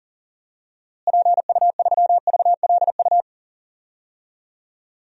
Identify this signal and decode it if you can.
Morse code, heard as a keyed tone, about 39 wpm, 700 Hz. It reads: PU3VLU
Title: DXCC-100 Call Signs - 6 Characters - Repeated 2x - 40wpm